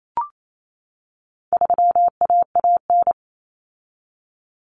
Morse code, heard as a keyed tone, about 28 wpm, 700 Hz. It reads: 3AAN
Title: Common Cut Numbers - 4-Digit 28wpm